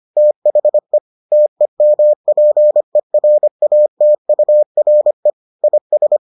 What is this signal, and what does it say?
Morse code, heard as a keyed tone, about 25 wpm, 600 Hz. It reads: THE TEMPERATURE IS